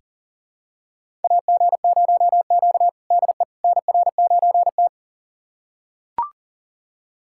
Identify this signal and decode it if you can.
Morse code, heard as a keyed tone, about 40 wpm, 700 Hz. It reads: AG0Q DE NR9T
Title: USA Callsign DE Callsign - 4 Characters - 40wpm